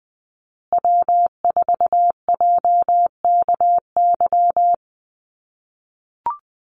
Morse code, heard as a keyed tone, about 20 wpm, 700 Hz. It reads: W4JKY
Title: USA Call Signs Repeated 2x 20wpm